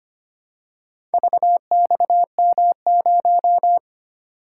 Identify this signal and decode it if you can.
Morse code, heard as a keyed tone, about 25 wpm, 700 Hz. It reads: VXM0
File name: practice / 25wpm / Four Letter-Number Combinations - 25wpm